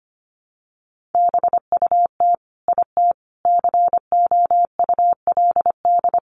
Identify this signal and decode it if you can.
Morse code, heard as a keyed tone, about 25 wpm, 700 Hz. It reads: BUT IT COULD
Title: Sentences from Top 1400 Words 25wpm